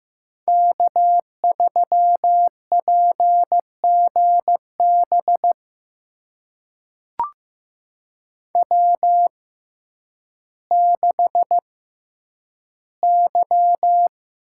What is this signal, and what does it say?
Morse code, heard as a keyed tone, about 15 wpm, spaced as for 9 wpm, 700 Hz. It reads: K3PGB W6Y
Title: USA Call Signs - 5 Characters - 6x Character Spacing - 15wpm